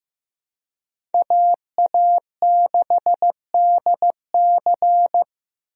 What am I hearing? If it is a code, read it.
Morse code, heard as a keyed tone, about 15 wpm, 700 Hz. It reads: AA6DC
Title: USA Call Signs - 5 Characters - 15wpm